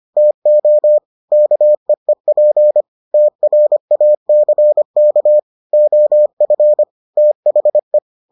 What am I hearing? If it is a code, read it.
Morse code, heard as a keyed tone, about 25 wpm, 600 Hz. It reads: TO KEEP TRACK OF THE